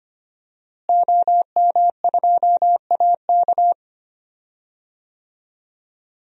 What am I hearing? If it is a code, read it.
Morse code, heard as a keyed tone, about 25 wpm, 700 Hz. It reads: OM2AK
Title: DXCC-100 Call Signs - 5 Characters - Repeated 2x - 25wpm